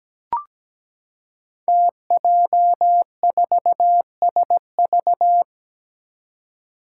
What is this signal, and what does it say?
Morse code, heard as a keyed tone, about 17 wpm, 700 Hz. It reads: TJ4SV
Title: Five Challenging Letter-Number Combinations 17wpm